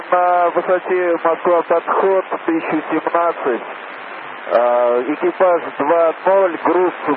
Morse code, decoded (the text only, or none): none